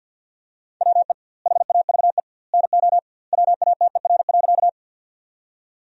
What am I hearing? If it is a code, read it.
Morse code, heard as a keyed tone, about 47 wpm, 700 Hz. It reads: WE HAVE NO WATER.
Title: 2x Word-Spacing - Sentences from Top 100 Words - 50wpm